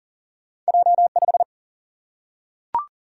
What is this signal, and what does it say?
Morse code, heard as a keyed tone, about 38 wpm, 700 Hz. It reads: J5
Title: Single-Letter Single-Number 40wpm